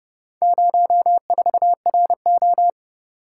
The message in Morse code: 04RO

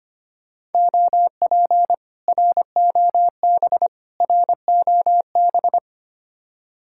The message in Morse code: OP ROB ROB

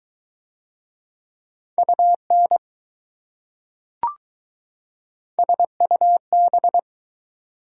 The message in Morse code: UN SUB

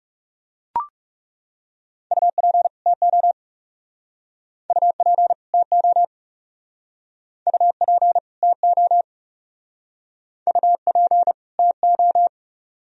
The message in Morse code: UP TO UP TO UP TO UP TO

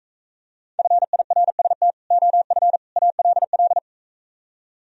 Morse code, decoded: FIRST OF ALL

